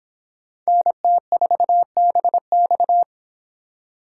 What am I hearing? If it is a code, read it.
Morse code, heard as a keyed tone, about 23 wpm, 700 Hz. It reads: NT4BX